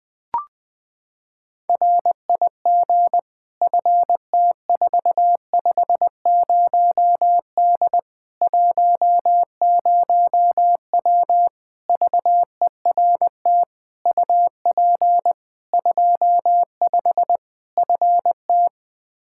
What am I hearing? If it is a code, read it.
Morse code, heard as a keyed tone, about 20 wpm, 700 Hz. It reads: RIG FT450D 10W VERT UP 25 FT
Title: Practice QSO - Exchange 2 - Info - 20wpm